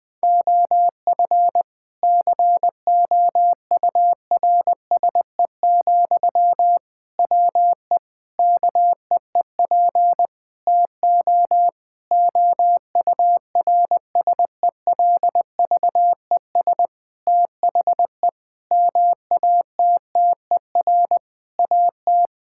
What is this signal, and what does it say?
Morse code, heard as a keyed tone, about 20 wpm, 700 Hz. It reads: OF COURSE, WE KEEP TO OURSELVES THE MATTER AT
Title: Sentences from Top 1600 Words 20wpm